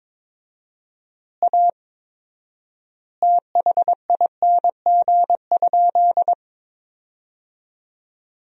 Morse code, heard as a keyed tone, about 22 wpm, 700 Hz. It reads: A THING?
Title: 4x Word-Spacing - Sentences from Top 200 Words - 22wpm